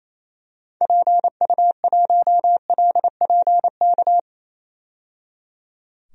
Morse code, heard as a keyed tone, about 28 wpm, 700 Hz. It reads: PU1LPK
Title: DXCC-100 Call Signs - 6 Characters - 28wpm